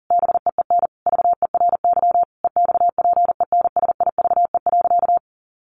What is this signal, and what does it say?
Morse code, heard as a keyed tone, about 40 wpm, 700 Hz. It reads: BEEN VERY EXPENSIVE.
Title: Sentences from Top 1500 Words 40wpm